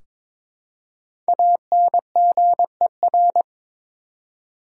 Morse code, heard as a keyed tone, about 22 wpm, 700 Hz. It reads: ANGER
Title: Five Letter Words - 22wpm